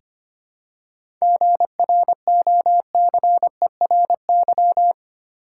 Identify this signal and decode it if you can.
Morse code, heard as a keyed tone, about 25 wpm, 700 Hz. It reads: GROCERY
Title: Seven Letter Words - 25wpm